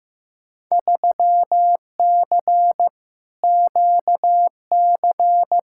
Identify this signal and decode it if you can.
Morse code, heard as a keyed tone, about 15 wpm, 700 Hz. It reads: 3C QC
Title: Field Day Exchanges Encore - 15wpm